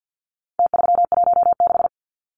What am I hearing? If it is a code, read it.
Morse code, heard as a keyed tone, about 49 wpm, 700 Hz. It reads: T316
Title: Numbers - Contest Style - 4-Digit 50wpm